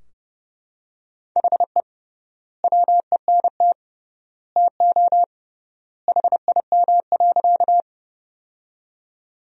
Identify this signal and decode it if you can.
Morse code, heard as a keyed tone, about 30 wpm, 700 Hz. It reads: HE WENT TO HIM.
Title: 3x Word-Spacing - Sentences from Top 200 Words - 30wpm